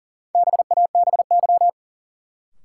Morse code, heard as a keyed tone, about 40 wpm, 700 Hz. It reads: BABY
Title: Sentences from Top 1300 Words - New Word Drill Alt - 40wpm